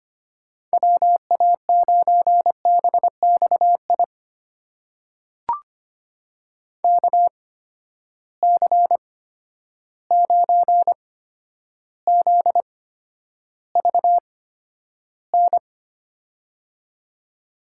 Morse code, heard as a keyed tone, about 25 wpm, spaced as for 11 wpm, 700 Hz. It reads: WA9BXI KC9ZVN
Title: USA Call Signs - 6 Characters - 8x Character Spacing - 25wpm